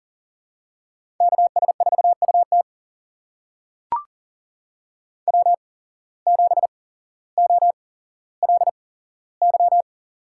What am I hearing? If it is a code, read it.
Morse code, heard as a keyed tone, about 38 wpm, spaced as for 17 wpm, 700 Hz. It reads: KS4UT W7OLY
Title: USA Call Signs - 8x Character Spacing - 40wpm